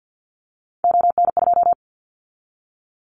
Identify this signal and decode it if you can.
Morse code, heard as a keyed tone, about 50 wpm, 700 Hz. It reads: ON2
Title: Callsign Prefixes for Top 100 DXCC - 50wpm